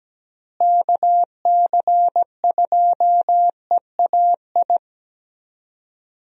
Morse code, heard as a keyed tone, about 17 wpm, 700 Hz. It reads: KC2EAI
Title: USA Call Signs - 6 Characters - Repeated 2x - 17wpm